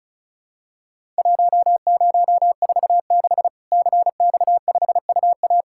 Morse code, heard as a keyed tone, about 36 wpm, 700 Hz. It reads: 1046 CX5UA
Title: Worked All Europe DX Contest - QTC Exchange - 35wpm